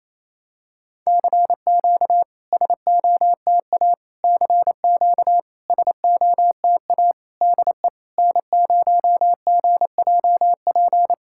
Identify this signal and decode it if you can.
Morse code, heard as a keyed tone, about 28 wpm, 700 Hz. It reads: CQ SOTA CQ SOTA DE N0GJP
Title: SOTA QSO Practice 28wpm